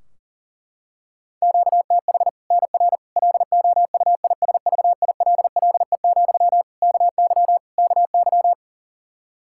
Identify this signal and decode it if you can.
Morse code, heard as a keyed tone, about 40 wpm, 700 Hz. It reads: QTH NR LOUISVILLE, KY KY
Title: K1USN SST QSO Practice 40wpm